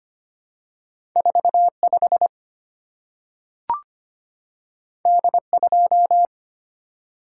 Morse code, heard as a keyed tone, about 25 wpm, 700 Hz. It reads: 45 D2
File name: Two Letter-Number Combinations - 25wpm